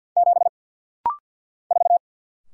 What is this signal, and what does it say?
Morse code, heard as a keyed tone, about 48 wpm, 700 Hz. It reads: 7 4
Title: Single Letter-Number - Rapid-Fire 50wpm